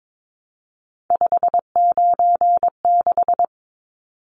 Morse code, heard as a keyed tone, about 22 wpm, 700 Hz. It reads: 596
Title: Numbers - 3-Digit 22wpm